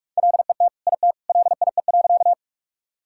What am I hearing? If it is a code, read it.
Morse code, heard as a keyed tone, about 46 wpm, 700 Hz. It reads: LET IT LIE.
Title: Sentences from Top 1600 Words 45wpm